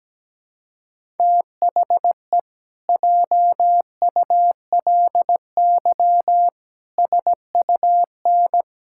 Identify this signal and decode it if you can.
Morse code, heard as a keyed tone, about 17 wpm, 700 Hz. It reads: THE JULY SUN
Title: Pangram Challenge v2 - 17wpm